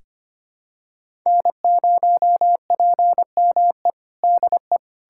Morse code, heard as a keyed tone, about 25 wpm, 700 Hz. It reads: N0PME DE
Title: USA Callsign DE Callsign - 5 Characters - 25wpm